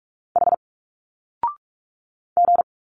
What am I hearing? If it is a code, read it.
Morse code, heard as a keyed tone, about 46 wpm, 700 Hz. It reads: H G
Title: Single Letters 45wpm